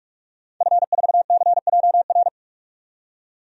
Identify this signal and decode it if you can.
Morse code, heard as a keyed tone, about 46 wpm, 700 Hz. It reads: F4CJR
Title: DXCC-100 Call Signs - 5 Characters - 45wpm